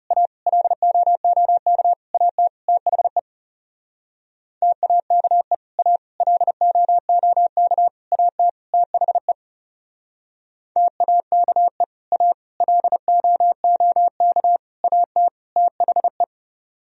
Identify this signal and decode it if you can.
Morse code, heard as a keyed tone, about 37 wpm, 700 Hz. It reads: A LOOK AT THE TAKE A LOOK AT THE TAKE A LOOK AT THE
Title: Speed-Racing - Sets of 5 Words 30wpm